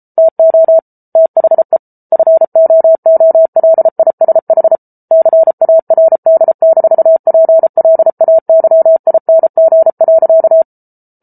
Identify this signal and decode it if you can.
Morse code, heard as a keyed tone, about 34 wpm, 650 Hz. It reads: TO THE FOOLISH CARD-PLAYING.